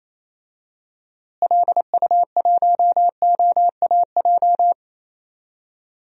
Morse code, heard as a keyed tone, about 27 wpm, 700 Hz. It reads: LU1OAJ
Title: DXCC-100 Call Signs - 6 Characters - Repeated 2x - 28wpm